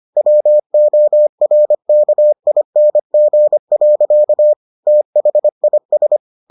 Morse code, heard as a keyed tone, about 25 wpm, 600 Hz. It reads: WORKING. THIS